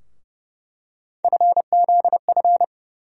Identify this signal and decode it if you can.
Morse code, heard as a keyed tone, about 29 wpm, 700 Hz. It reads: FZF
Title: USA Call Sign Postfixes - 3 Letters 30wpm